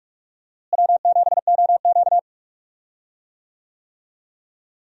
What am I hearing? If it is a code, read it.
Morse code, heard as a keyed tone, about 46 wpm, 700 Hz. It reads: W7OQ
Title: USA Call Signs - 4 Characters - Repeated 3x - 45wpm